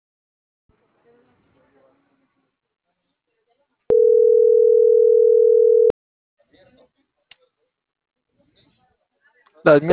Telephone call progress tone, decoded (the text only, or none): ringback tone